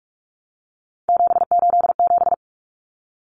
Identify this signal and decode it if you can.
Morse code, heard as a keyed tone, about 44 wpm, 700 Hz. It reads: 787
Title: Common Cut Numbers - 3-Digit 45wpm